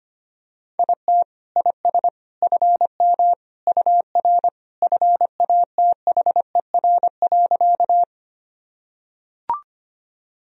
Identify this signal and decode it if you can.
Morse code, heard as a keyed tone, about 25 wpm, 700 Hz. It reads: IT IS FM UR FATHER.
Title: Sentences from Top 300 Words and Abbreviations 25wpm